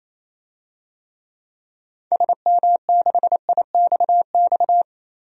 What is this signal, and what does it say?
Morse code, heard as a keyed tone, about 28 wpm, 700 Hz. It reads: SM6IXX
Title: DXCC-100 Call Signs - 6 Characters - 28wpm